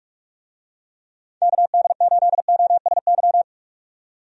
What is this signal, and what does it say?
Morse code, heard as a keyed tone, about 46 wpm, 700 Hz. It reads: KD8OSY
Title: USA Call Signs - 6 Characters - 45wpm